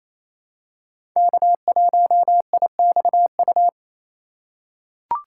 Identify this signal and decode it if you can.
Morse code, heard as a keyed tone, about 27 wpm, 700 Hz. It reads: K1IXU
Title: USA Call Signs - 5 Characters - 8x Character Spacing - 28wpm